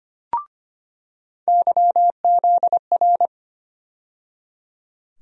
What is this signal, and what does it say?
Morse code, heard as a keyed tone, about 25 wpm, 700 Hz. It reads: YZR
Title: USA Call Sign Postfixes - 3 Letters 25wpm